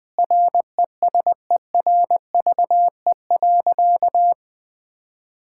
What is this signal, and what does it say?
Morse code, heard as a keyed tone, about 20 wpm, 700 Hz. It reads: RESERVE.
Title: Sentences from Top 1000 Words - Review Part 6 - 20wpm